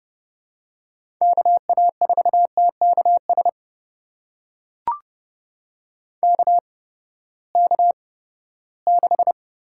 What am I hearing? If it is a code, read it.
Morse code, heard as a keyed tone, about 30 wpm, spaced as for 17 wpm, 700 Hz. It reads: KA4TKS KK6
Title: USA Call Signs - 8x Character Spacing - 30wpm